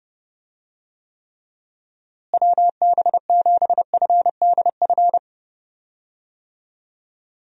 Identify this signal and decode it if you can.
Morse code, heard as a keyed tone, about 30 wpm, 700 Hz. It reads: WB7FDF
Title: USA Call Signs - 6 Characters - Repeated 2x - 30wpm